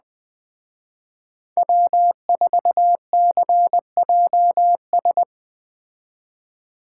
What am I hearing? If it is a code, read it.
Morse code, heard as a keyed tone, about 20 wpm, 700 Hz. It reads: W4CJS